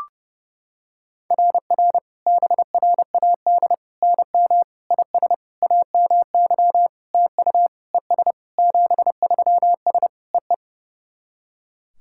Morse code, heard as a keyed tone, about 30 wpm, 700 Hz. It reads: RR BRAD NM IS AMY TU ES 73S EE
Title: SOTA QSO Practice 30wpm